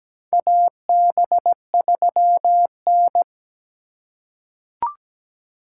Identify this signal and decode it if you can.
Morse code, heard as a keyed tone, about 17 wpm, 700 Hz. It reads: AB3N
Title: USA Call Signs - 4 Characters - 3x Character Spacing - 17wpm